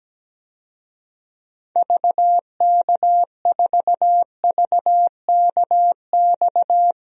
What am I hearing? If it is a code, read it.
Morse code, heard as a keyed tone, about 17 wpm, 700 Hz. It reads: VK4VKX
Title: DXCC-100 Call Signs - 6 Characters - Repeated 2x - 17wpm